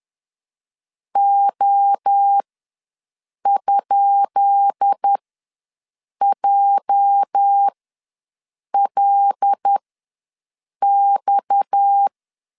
Morse code, heard as a keyed tone, about 10 wpm, spaced as for 6 wpm, 800 Hz. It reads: O?JLX
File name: LEZIONI / 10 SIGNS / 10B INT ws 10wpm 4134